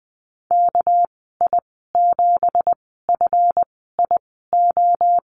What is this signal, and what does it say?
Morse code, heard as a keyed tone, about 19 wpm, spaced as for 14 wpm, 700 Hz. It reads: KI7FIO
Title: 231 Lesson 37g - Letter x - Call Signs 20wpm